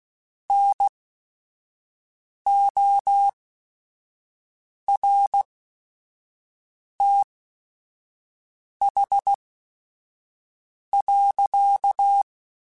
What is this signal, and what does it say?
Morse code, heard as a keyed tone, about 16 wpm, spaced as for 5 wpm, 800 Hz. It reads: NORTH.
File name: Weather Sets / Weather Sets-mp3s / Weather 4 28 @5